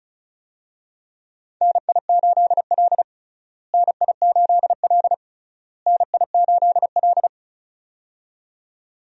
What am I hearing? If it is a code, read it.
Morse code, heard as a keyed tone, about 36 wpm, 700 Hz. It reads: NI8L NI8L NI8L
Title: USA Call Signs - 4 Characters - Repeated 3x - 35wpm